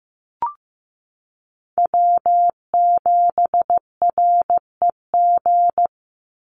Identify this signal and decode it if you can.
Morse code, heard as a keyed tone, about 15 wpm, 700 Hz. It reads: W7REG
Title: USA Call Signs 15wpm